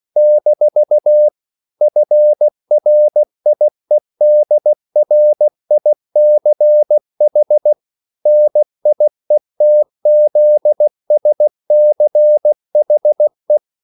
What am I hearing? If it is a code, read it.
Morse code, heard as a keyed tone, about 16 wpm, 600 Hz. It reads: - FRIEDRICH NIETZSCHE
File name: CW / Practice / EnglishQuotations0116